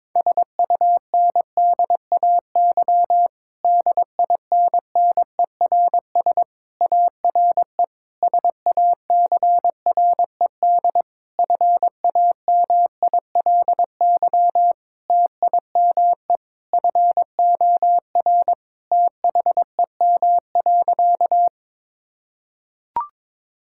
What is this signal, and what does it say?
Morse code, heard as a keyed tone, about 22 wpm, 700 Hz. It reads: SUNDAY DINNERS ARE SACRED FAMILY TIME FOR THEM.
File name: Difficult Sentences with Hints - 22wpm